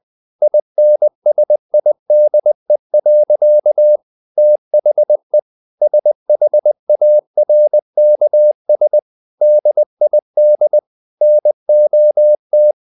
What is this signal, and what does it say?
Morse code, heard as a keyed tone, about 20 wpm, 600 Hz. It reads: INSIDE. THE SHARKS DID NOT
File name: the old man and the sea part 2 20wpm